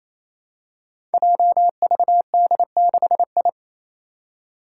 Morse code, heard as a keyed tone, about 28 wpm, 700 Hz. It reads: JVD6I
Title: Five Challenging Letter-Number Combinations 28wpm